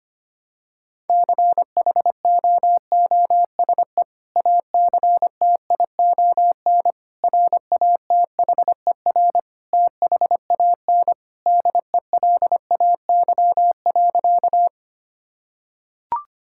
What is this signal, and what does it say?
Morse code, heard as a keyed tone, about 25 wpm, 700 Hz. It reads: CHOOSE ACTION RATHER THAN DELAY.
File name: practice / 25wpm / Sentences with 2-Word Phrases - 25wpm